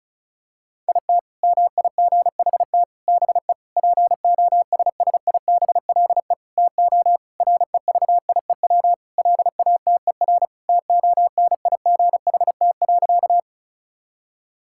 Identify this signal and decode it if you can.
Morse code, heard as a keyed tone, about 36 wpm, 700 Hz. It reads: IT MIGHT BE POSSIBLE TO REVIEW LATER TONIGHT.
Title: Sentences from Top 1000 Words - Review Part 6 - 35wpm